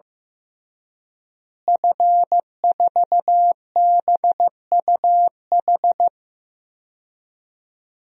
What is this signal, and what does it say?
Morse code, heard as a keyed tone, about 15 wpm, 700 Hz. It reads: F4BUH
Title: DXCC-100 Call Signs - 5 Characters - Repeated 3x - 15wpm